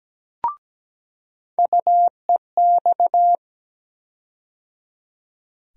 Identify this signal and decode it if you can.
Morse code, heard as a keyed tone, about 17 wpm, 700 Hz. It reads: UEX
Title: USA Call Sign Postfixes - 3 Letters 17wpm